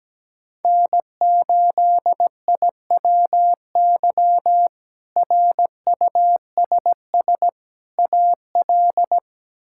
Morse code, heard as a keyed tone, about 17 wpm, 700 Hz. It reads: N8IWY RUSS AL
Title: K1USN SST QSO Practice - 17wpm